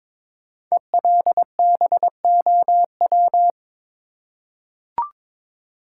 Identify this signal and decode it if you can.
Morse code, heard as a keyed tone, about 22 wpm, 700 Hz. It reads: ELBOW
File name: 5 Letter Words - First 3 Letters Given - 22wpm